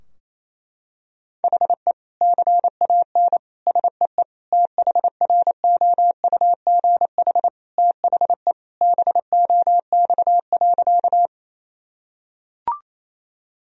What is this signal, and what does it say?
Morse code, heard as a keyed tone, about 28 wpm, 700 Hz. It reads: HE CAN SEE THROUGH THE BOX.
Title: Sentences from Top 1000 Words - Review Part 2 - 28wpm